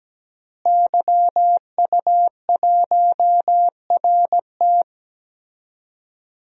Morse code, heard as a keyed tone, about 17 wpm, 700 Hz. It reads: YU1RT
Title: DXCC-100 Call Signs - 5 Characters - Repeated 3x - 17wpm